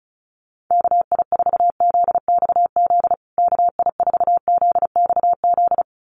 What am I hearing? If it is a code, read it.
Morse code, heard as a keyed tone, about 34 wpm, 700 Hz. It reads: KI4ZXZ KI4ZXZ
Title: USA Call Signs Repeated 2x 35wpm